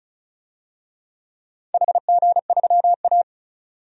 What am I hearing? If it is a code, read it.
Morse code, heard as a keyed tone, about 36 wpm, 700 Hz. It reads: HG3A
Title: DXCC-100 Call Signs - 4 Characters - 35wpm